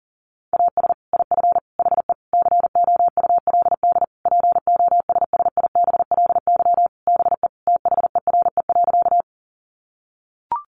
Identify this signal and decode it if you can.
Morse code, heard as a keyed tone, about 40 wpm, 700 Hz. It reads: AS IF HE COULD POSSIBLY BE THERE.
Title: Sentences from Top 1300 Words 40wpm